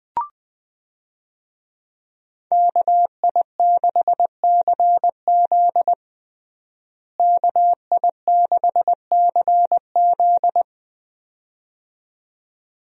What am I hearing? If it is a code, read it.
Morse code, heard as a keyed tone, about 21 wpm, 700 Hz. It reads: KI6CZ KI6CZ
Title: USA Call Signs - 5 Characters - Repeated 2x - 20wpm